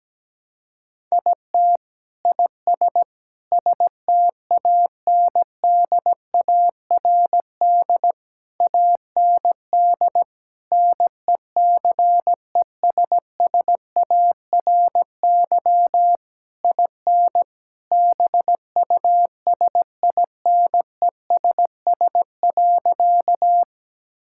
Morse code, heard as a keyed tone, about 17 wpm, 700 Hz. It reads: IT IS STANDARD AND NECESSARY IN BUSINESS.
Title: Sentences from Top 1000 Words - Review Part 6 - 17wpm